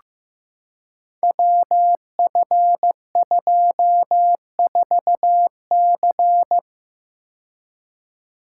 Morse code, heard as a keyed tone, about 15 wpm, 700 Hz. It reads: WF24C
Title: Code Groups - 5 Letter:Numbers - 15wpm